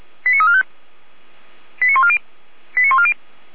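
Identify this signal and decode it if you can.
Five-tone selective call (ZVEI), heard as quick runs of five tones, tones of about 70 ms. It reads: 89345 89140 89140
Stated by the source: ZVEI five-tone call received via SDR.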